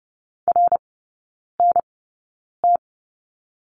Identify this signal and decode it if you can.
Morse code, heard as a keyed tone, about 30 wpm, spaced as for 9 wpm, 700 Hz. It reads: RNT
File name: Lesson 17e - Letter u - Three Characters 30wpm